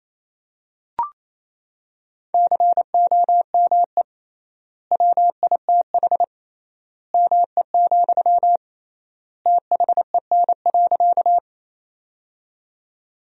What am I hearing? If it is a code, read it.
Morse code, heard as a keyed tone, about 28 wpm, 700 Hz. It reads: COME WITH ME, THEN.
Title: 3x Word-Spacing - Sentences from Top 200 Words - 28wpm